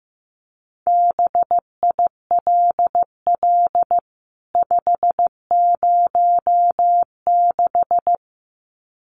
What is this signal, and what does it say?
Morse code, heard as a keyed tone, about 15 wpm, 700 Hz. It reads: BILL 506
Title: CWT Exchange 15wpm